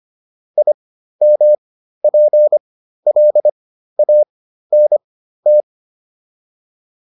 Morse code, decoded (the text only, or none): IMPLANT